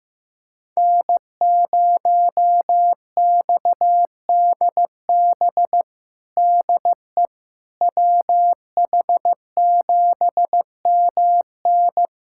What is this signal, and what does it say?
Morse code, heard as a keyed tone, about 15 wpm, 700 Hz. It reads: N0XDB DE WH7MN
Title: USA Callsign DE Callsign - 5 Characters - 15wpm